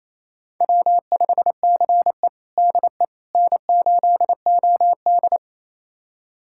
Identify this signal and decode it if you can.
Morse code, heard as a keyed tone, about 27 wpm, 700 Hz. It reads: W5CE DE N8OD
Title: USA Callsign DE Callsign - 4 Characters - 28wpm